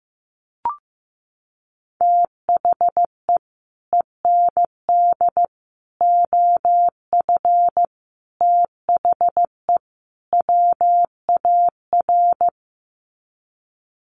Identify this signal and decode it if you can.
Morse code, heard as a keyed tone, about 15 wpm, 700 Hz. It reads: THE END OF THE WAR
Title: Sets of 5 Words 15wpm